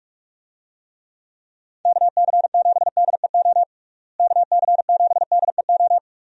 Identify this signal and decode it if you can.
Morse code, heard as a keyed tone, about 46 wpm, 700 Hz. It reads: KC7DEO KC7DEO
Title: USA Call Signs - 6 Characters - Repeated 2x - 45wpm